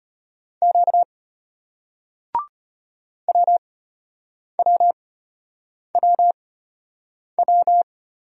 Morse code, heard as a keyed tone, about 33 wpm, spaced as for 8 wpm, 700 Hz. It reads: Q WWWW